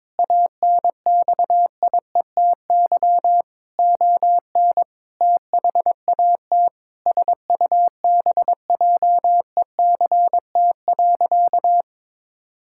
Morse code, heard as a keyed tone, about 22 wpm, 700 Hz. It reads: ANXIETY ON THAT SUBJECT.
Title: Sentences from Top 1600 Words 22wpm